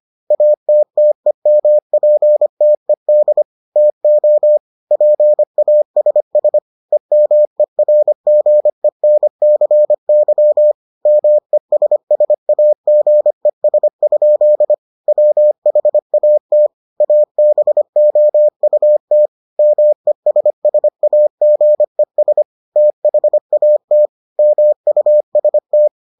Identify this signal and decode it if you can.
Morse code, heard as a keyed tone, about 25 wpm, 600 Hz. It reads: ATTEMPTED TO PASS EMERGENCY MESSAGES? WHAT ABOUT MESSAGES THAT MUST